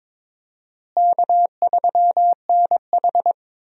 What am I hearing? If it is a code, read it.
Morse code, heard as a keyed tone, about 22 wpm, 700 Hz. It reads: K3NH
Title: USA Call Signs - 4 Characters - Repeated 3x - 22wpm